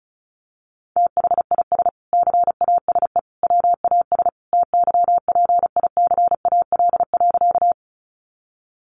Morse code, heard as a keyed tone, about 34 wpm, 700 Hz. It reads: THIS CASE WAS TYPICAL.
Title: Sentences from Top 1100 Words 35wpm